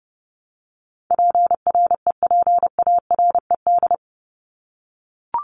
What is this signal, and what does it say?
Morse code, heard as a keyed tone, about 30 wpm, 700 Hz. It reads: PREPARED
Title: ED Suffix - Top 100 Words with Base Words 30wpm